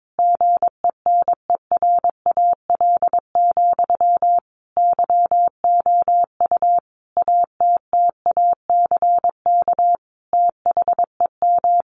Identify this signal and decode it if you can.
Morse code, heard as a keyed tone, about 22 wpm, 700 Hz. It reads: GENERAL, YOU ATTACK THEM
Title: Sentences from Top 1500 Words 22wpm